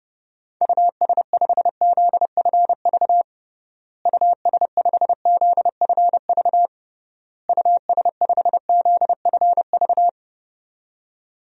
Morse code, heard as a keyed tone, about 32 wpm, 700 Hz. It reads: US5ZFV US5ZFV US5ZFV
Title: DXCC-100 Call Signs - 6 Characters - Repeated 3x - 30wpm